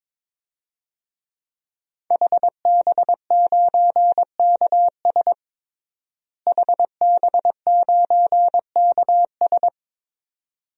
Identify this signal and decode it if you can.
Morse code, heard as a keyed tone, about 22 wpm, 700 Hz. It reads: HB9KS HB9KS
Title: DXCC-100 Call Signs - 5 Characters - Repeated 2x - 22wpm